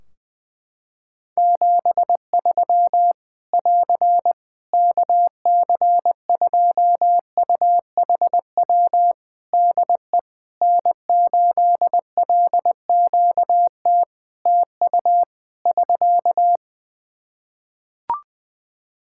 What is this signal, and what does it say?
Morse code, heard as a keyed tone, about 20 wpm, 700 Hz. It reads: 73 <AR> KC2UHW DE N8LQT TU <SK>
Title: Practice QSO - Exchange 4 - Ending - 20wpm